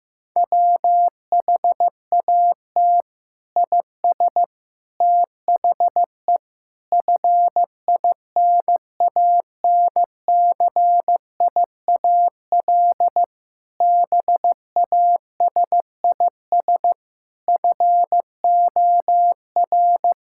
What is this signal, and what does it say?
Morse code, heard as a keyed tone, about 15 wpm, 700 Hz. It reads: WHAT IS THE FINANCIAL BASIS FOR
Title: Sentences from Top 1000 Words - Review Part 5 - 15wpm